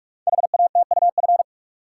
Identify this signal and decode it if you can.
Morse code, heard as a keyed tone, about 44 wpm, 690 Hz. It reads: HATUF
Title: Code Groups - 5 Letters - 45wpm